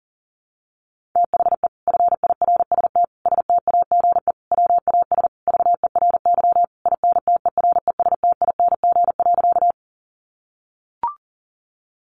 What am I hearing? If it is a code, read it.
Morse code, heard as a keyed tone, about 40 wpm, 700 Hz. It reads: THE FIRST STAGE WAS VERY INTERESTING.
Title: Sentences from Top 1200 Words 40wpm